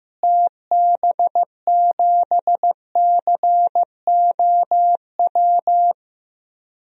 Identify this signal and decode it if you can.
Morse code, heard as a keyed tone, about 16 wpm, 700 Hz. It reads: TB7COW
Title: DXCC-100 Call Signs - 6 Characters - Repeated 2x - 15wpm